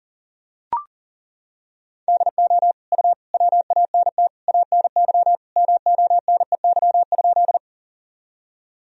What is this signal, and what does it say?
Morse code, heard as a keyed tone, about 39 wpm, 700 Hz. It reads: DO U WANT ANY MONEY?
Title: Sentences from Top 400 Words and Abbreviations 40wpm